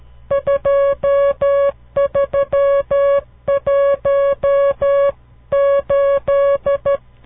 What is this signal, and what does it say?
Morse code, heard as a keyed tone, about 12 wpm, 560 Hz. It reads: 2318
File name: CW Learning Programs / farnsworth / unit 5